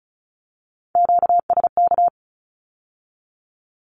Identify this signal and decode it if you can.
Morse code, heard as a keyed tone, about 36 wpm, 700 Hz. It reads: QSK
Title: Q-Codes with Spoken Meaning - 35wpm